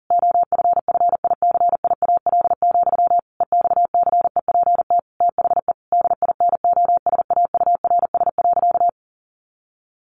Morse code, heard as a keyed tone, about 40 wpm, 700 Hz. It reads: OFFICIAL, EXCEPT THE DINOSAURS.
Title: A Sound Of Thunder 40wpm